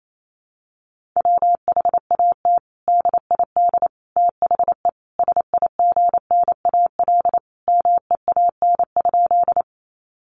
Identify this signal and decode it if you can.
Morse code, heard as a keyed tone, about 28 wpm, 700 Hz. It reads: WHAT DID THE SIGNAL MEAN?
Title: Sentences from Top 1400 Words 28wpm